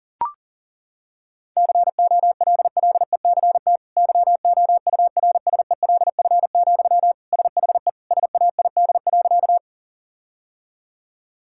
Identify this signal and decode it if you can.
Morse code, heard as a keyed tone, about 40 wpm, 700 Hz. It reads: COLLECT YOURSELF, SHE SAID.
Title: Sentences from Top 1600 Words 40wpm